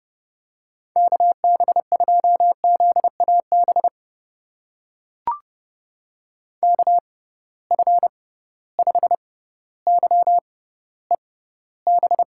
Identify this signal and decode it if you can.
Morse code, heard as a keyed tone, about 30 wpm, spaced as for 15 wpm, 700 Hz. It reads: KB2ZAB KF5YEB
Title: USA Call Signs - 6x Character Spacing - 30wpm